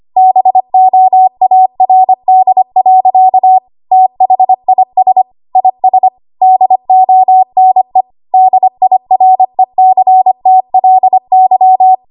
Morse code, heard as a keyed tone, about 25 wpm, 750 Hz. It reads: BOARD. THIS IS DONE DIRECTLY
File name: Archive / 25 WPM / 25WPM